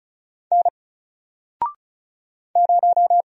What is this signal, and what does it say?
Morse code, heard as a keyed tone, about 34 wpm, 700 Hz. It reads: N 0
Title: Single Letter-Number Alt 35wpm